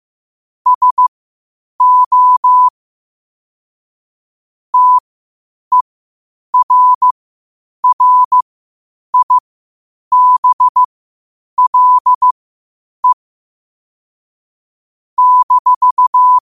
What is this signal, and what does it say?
Morse code, heard as a keyed tone, about 15 wpm, spaced as for 8 wpm, 1000 Hz. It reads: SO TERRIBLE -
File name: qotdfiles / 8wpm